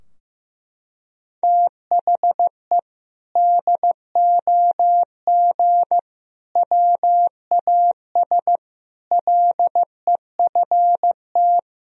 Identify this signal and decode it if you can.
Morse code, heard as a keyed tone, about 15 wpm, 700 Hz. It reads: THE DOG WAS LEFT
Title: Sentences from Top 1000 Words - Review Part 2 - 15wpm